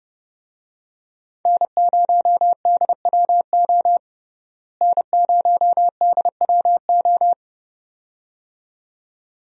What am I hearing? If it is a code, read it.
Morse code, heard as a keyed tone, about 30 wpm, 700 Hz. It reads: N0DWO N0DWO
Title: USA Call Signs - 5 Characters - Repeated 2x - 30wpm